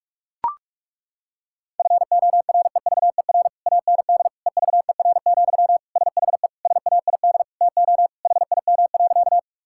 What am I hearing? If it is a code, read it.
Morse code, heard as a keyed tone, about 41 wpm, 700 Hz. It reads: FOREVER AND EVER, SHE SAID TO HIM.
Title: Sentences from Top 1600 Words 45wpm